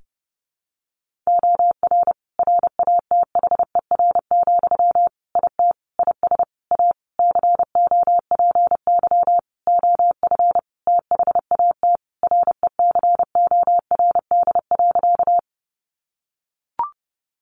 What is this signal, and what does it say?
Morse code, heard as a keyed tone, about 30 wpm, 700 Hz. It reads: OR RATHER, IT IS A COPY OF THAT RECORD.
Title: Sentences from Top 1400 Words 30wpm